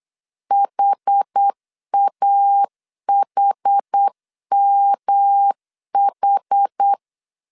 Morse code, heard as a keyed tone, about 8 wpm, 800 Hz. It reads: HAHMH
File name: CW / DET / 1 3 1500C S B 8wpm 4141